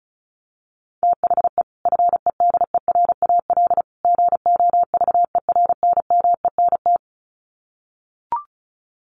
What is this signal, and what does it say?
Morse code, heard as a keyed tone, about 34 wpm, 700 Hz. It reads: THE FEDERAL GOVERNMENT
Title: Sets of 3 Words 35wpm